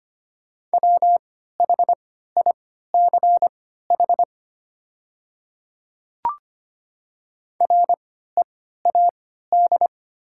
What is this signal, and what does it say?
Morse code, heard as a keyed tone, about 25 wpm, spaced as for 14 wpm, 700 Hz. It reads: WHICH READ
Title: 3x Character Spacing - Top 500 Words - 25wpm